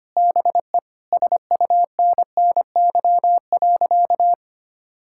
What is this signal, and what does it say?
Morse code, heard as a keyed tone, about 25 wpm, 700 Hz. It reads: BE SUNNY.
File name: practice / 25wpm / Sentences with 3-Word Phrases - 25wpm